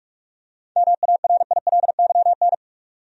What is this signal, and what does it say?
Morse code, heard as a keyed tone, about 46 wpm, 700 Hz. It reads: MARILYN
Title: USA Common Names - 45wpm